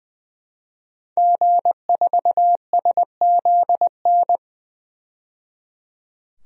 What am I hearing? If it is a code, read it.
Morse code, heard as a keyed tone, about 20 wpm, 700 Hz. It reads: G4SZN
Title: DXCC-100 Call Signs - 5 Characters - 20wpm